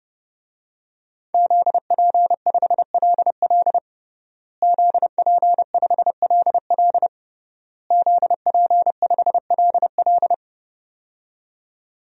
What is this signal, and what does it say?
Morse code, heard as a keyed tone, about 30 wpm, 700 Hz. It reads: ZP5LL ZP5LL ZP5LL
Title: DXCC-100 Call Signs - 5 Characters - Repeated 3x - 30wpm